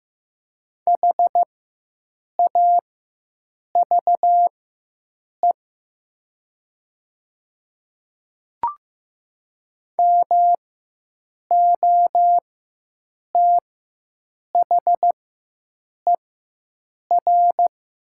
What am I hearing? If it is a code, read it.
Morse code, heard as a keyed tone, about 15 wpm, spaced as for 7 wpm, 700 Hz. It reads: HAVE MOTHER
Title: 4x Character Spacing - Top 500 Words - 15wpm